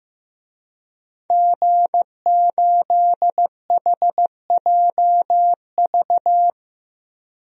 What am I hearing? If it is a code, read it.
Morse code, heard as a keyed tone, about 15 wpm, 700 Hz. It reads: G8HJV
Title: DXCC-100 Call Signs - 5 Characters - 15wpm